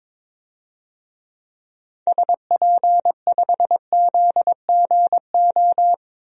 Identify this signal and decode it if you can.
Morse code, heard as a keyed tone, about 22 wpm, 700 Hz. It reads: SP5ZGO
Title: DXCC-100 Call Signs - 6 Characters - 22wpm